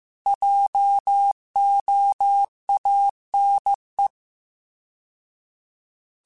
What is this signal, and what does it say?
Morse code, heard as a keyed tone, about 15 wpm, 800 Hz. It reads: JOANE
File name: Alphabetical / Alphabetical mp3s / Js @13